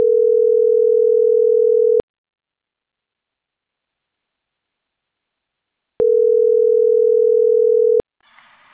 Ringback tone (North America), 440 + 480 Hz.